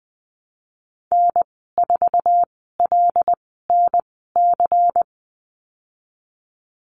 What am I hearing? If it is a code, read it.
Morse code, heard as a keyed tone, about 19 wpm, spaced as for 14 wpm, 700 Hz. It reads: N4LNC
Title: 179 Lesson 29g - Slash - Call Signs 20wpm